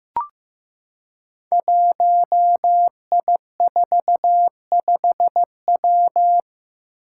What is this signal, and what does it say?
Morse code, heard as a keyed tone, about 15 wpm, 700 Hz. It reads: 1I45W
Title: Five Challenging Letter-Number Combinations 15wpm